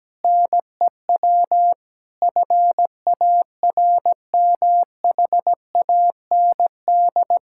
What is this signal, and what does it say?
Morse code, heard as a keyed tone, about 17 wpm, 700 Hz. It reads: NEW FARMHAND
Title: Pangram Challenge v2 - 17wpm